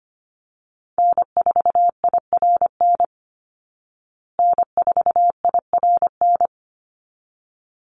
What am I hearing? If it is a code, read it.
Morse code, heard as a keyed tone, about 25 wpm, 700 Hz. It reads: N4IRN N4IRN
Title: Call Signs Repeated 3x - 25wpm